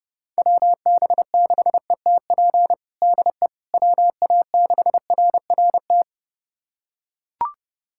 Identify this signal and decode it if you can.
Morse code, heard as a keyed tone, about 31 wpm, 700 Hz. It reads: WB6ETP DE WA6RRT
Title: USA Callsign DE Callsign - 6 Characters - 30wpm